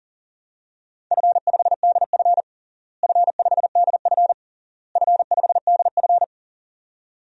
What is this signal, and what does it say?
Morse code, heard as a keyed tone, about 37 wpm, 700 Hz. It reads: F5DF F5DF F5DF
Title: DXCC-100 Call Signs - 4 Characters - Repeated 3x - 40wpm